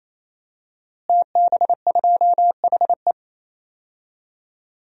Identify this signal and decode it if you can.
Morse code, heard as a keyed tone, about 28 wpm, 700 Hz. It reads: TB2HE
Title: DXCC-100 Call Signs - 5 Characters - Repeated 3x - 28wpm